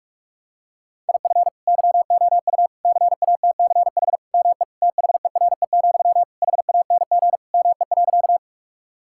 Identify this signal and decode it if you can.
Morse code, heard as a keyed tone, about 41 wpm, 700 Hz. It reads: IF YOU CATCH ME THERE, HANG ME.
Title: Sentences from Top 1600 Words 45wpm